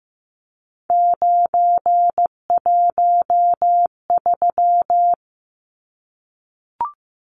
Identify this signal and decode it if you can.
Morse code, heard as a keyed tone, about 15 wpm, 700 Hz. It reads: 913
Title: Numbers - 3-Digit 15wpm